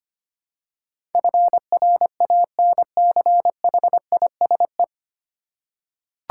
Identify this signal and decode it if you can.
Morse code, heard as a keyed tone, about 25 wpm, 700 Hz. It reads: FRANCHISE
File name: 9 Letter Words - First 3 Letters Given - 25wpm